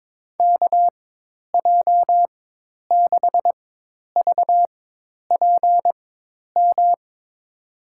Morse code, heard as a keyed tone, about 21 wpm, spaced as for 10 wpm, 700 Hz. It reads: KJ6VPM
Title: USA Call Signs - 6 Characters - 4x Character Spacing - 22wpm